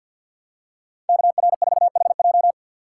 Morse code, heard as a keyed tone, about 50 wpm, 700 Hz. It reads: KR4HJ
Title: USA Call Signs - 5 Characters - 2x Character Spacing - 50wpm